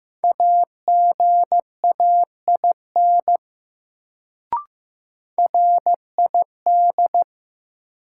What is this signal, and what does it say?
Morse code, heard as a keyed tone, about 15 wpm, 700 Hz. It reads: AGAIN RID
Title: Sets of 4 Words - New Word Drill - 15wpm